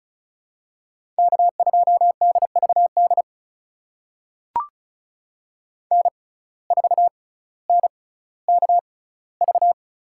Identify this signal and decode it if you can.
Morse code, heard as a keyed tone, about 34 wpm, spaced as for 18 wpm, 700 Hz. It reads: K2DVD N4NKV
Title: USA Call Signs - 5 Characters - 6x Character Spacing - 35wpm